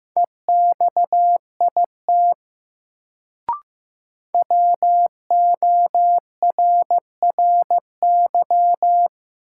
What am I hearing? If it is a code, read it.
Morse code, heard as a keyed tone, about 15 wpm, 700 Hz. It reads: EXIT WORRY
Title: Sentences from Top 1100 Words - New Word Drill - 15wpm